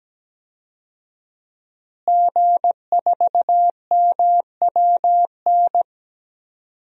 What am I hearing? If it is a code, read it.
Morse code, heard as a keyed tone, about 17 wpm, 700 Hz. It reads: G4MWN
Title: DXCC-100 Call Signs - 5 Characters - Repeated 3x - 17wpm